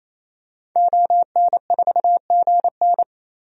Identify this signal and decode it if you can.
Morse code, heard as a keyed tone, about 28 wpm, 700 Hz. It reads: ON4GN
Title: DXCC-100 Call Signs - 5 Characters - 28wpm